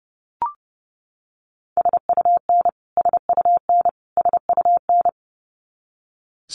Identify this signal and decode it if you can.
Morse code, heard as a keyed tone, about 30 wpm, 700 Hz. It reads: SUN SUN SUN
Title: Top 300 Words Repeated 30wpm